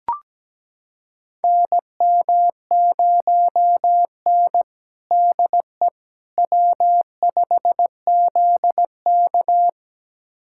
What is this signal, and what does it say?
Morse code, heard as a keyed tone, about 17 wpm, 700 Hz. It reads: NM0N DE W5ZK
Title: USA Callsign DE Callsign - 4 Characters - 17wpm